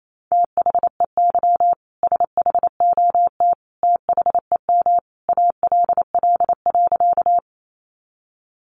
Morse code, heard as a keyed tone, about 28 wpm, 700 Hz. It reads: THEY SHOT THEM ALL.
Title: Sentences from Top 1100 Words 28wpm